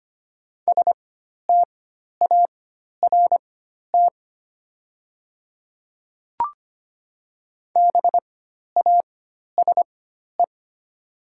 Morse code, heard as a keyed tone, about 25 wpm, spaced as for 12 wpm, 700 Hz. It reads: START BASE
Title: 4x Character Spacing - Top 300 Words - 25wpm